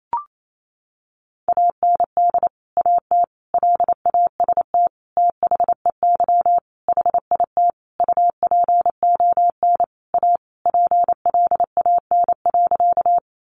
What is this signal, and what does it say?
Morse code, heard as a keyed tone, about 28 wpm, 700 Hz. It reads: AND AT LAST THEY HIT UPON A PLAN.
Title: Sentences from Top 1200 Words 28wpm